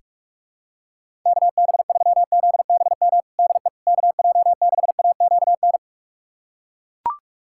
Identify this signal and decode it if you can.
Morse code, heard as a keyed tone, about 41 wpm, 700 Hz. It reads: KB3ZBM DE KJ6AQN
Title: USA Callsign DE Callsign - 6 Characters - 45wpm